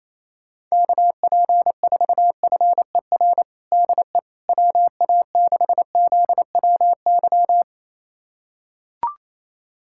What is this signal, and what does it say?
Morse code, heard as a keyed tone, about 28 wpm, 700 Hz. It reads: KP4FER DE WA6ZWY
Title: USA Callsign DE Callsign - 6 Characters - 28wpm